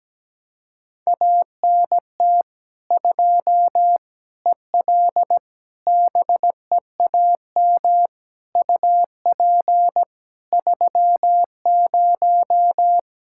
Morse code, heard as a keyed tone, about 17 wpm, 700 Hz. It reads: ANT 2 EL BEAM UP 30
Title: Practice QSO - Exchange 2 - Info - 17wpm